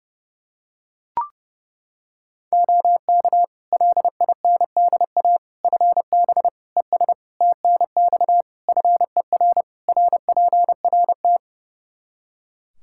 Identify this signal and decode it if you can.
Morse code, heard as a keyed tone, about 30 wpm, 700 Hz. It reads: OK LINDA FB ES TNX FER RPRT
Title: Practice QSO - Exchange 2 - Info - 30wpm